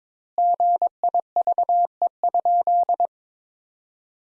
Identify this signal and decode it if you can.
Morse code, heard as a keyed tone, about 22 wpm, 700 Hz. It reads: GIVE?
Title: Sentences Top 200 Words - 22wpm